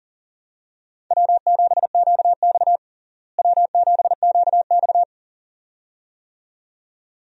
Morse code, heard as a keyed tone, about 40 wpm, 700 Hz. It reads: W7QX W7QX
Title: USA Call Signs - 4 Characters - Repeated 2x - 40wpm